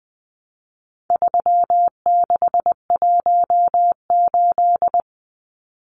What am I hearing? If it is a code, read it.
Morse code, heard as a keyed tone, about 20 wpm, 700 Hz. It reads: 3618
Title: Numbers - 4-Digit 20wpm